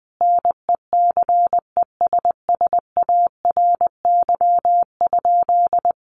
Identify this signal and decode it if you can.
Morse code, heard as a keyed tone, about 19 wpm, 700 Hz. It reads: NECESSARY?
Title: Sentences from Top 800 Words 20wpm